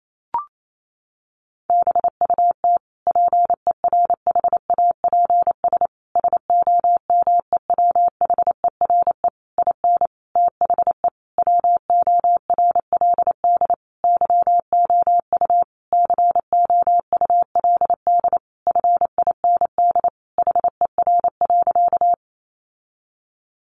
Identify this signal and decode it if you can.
Morse code, heard as a keyed tone, about 28 wpm, 700 Hz. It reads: BUT PERHAPS SOMEWHERE IN THE WORLD YOU COULD FIND HER.
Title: Sentences from Top 1400 Words 28wpm